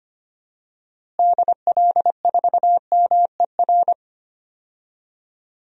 Morse code, heard as a keyed tone, about 25 wpm, 700 Hz. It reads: DL4MER
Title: DXCC-100 Call Signs - 6 Characters - Repeated 2x - 25wpm